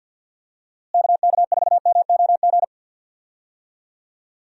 Morse code, heard as a keyed tone, about 51 wpm, 700 Hz. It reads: KK4MOG
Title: USA Call Signs - 6 Characters - Repeated 3x - 50wpm